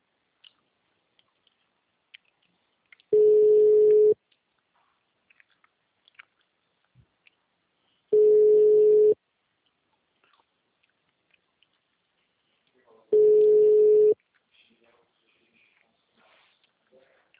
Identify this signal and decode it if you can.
Ringback tone (Europe, 425 Hz).